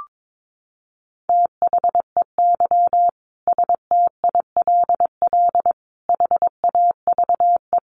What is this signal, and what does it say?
Morse code, heard as a keyed tone, about 22 wpm, 700 Hz. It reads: THEY STILL HAVE
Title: Sentences from Top 1000 Words 22wpm